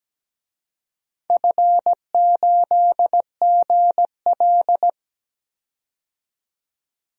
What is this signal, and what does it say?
Morse code, heard as a keyed tone, about 17 wpm, 700 Hz. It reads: F8GL
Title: DXCC-100 Call Signs - 4 Characters - 17wpm